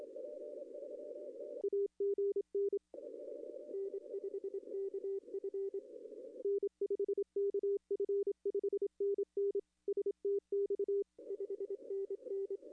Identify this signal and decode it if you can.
Morse code, heard as a keyed tone, about 26 wpm, 410 Hz. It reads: AGN N5KF N5KF 5NN STX 5NN
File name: contests / arrl-160-k5tr-op-w5tn -0200